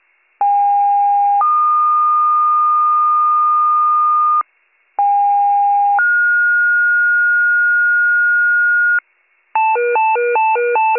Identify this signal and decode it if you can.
Two-tone page, heard as a one-second tone, then a three-second tone, about 810 then 1220 Hz, about 810 then 1430 Hz, with noise. It is heard from 0.4 to 9.0 s.